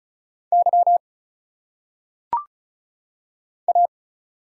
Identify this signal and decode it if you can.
Morse code, heard as a keyed tone, about 34 wpm, 700 Hz. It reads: Y A
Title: Single Letters - Phonetic Pronunciation 35wpm